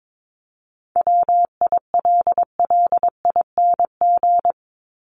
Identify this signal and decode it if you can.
Morse code, heard as a keyed tone, about 22 wpm, 700 Hz. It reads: WILLING
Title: Top 751-1000 Words 22wpm